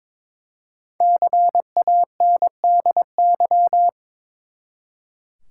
Morse code, heard as a keyed tone, about 22 wpm, 700 Hz. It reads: CANDY
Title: Five Letter Words - 22wpm